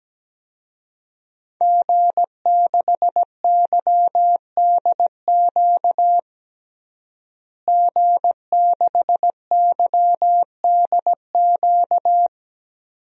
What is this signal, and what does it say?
Morse code, heard as a keyed tone, about 17 wpm, 700 Hz. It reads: G6YDQ G6YDQ
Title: DXCC-100 Call Signs - 5 Characters - Repeated 2x - 17wpm